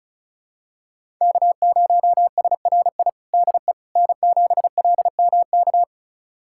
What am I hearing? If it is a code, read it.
Morse code, heard as a keyed tone, about 34 wpm, 700 Hz. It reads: K0SRI DE N7LMK
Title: USA Callsign DE Callsign - 5 Characters - 35wpm